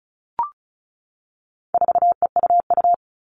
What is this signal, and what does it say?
Morse code, heard as a keyed tone, about 34 wpm, 700 Hz. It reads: 4EUU
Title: All Cut Numbers - 4-Digit 35wpm